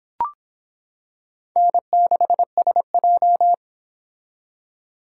Morse code, heard as a keyed tone, about 25 wpm, 700 Hz. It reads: N6SJ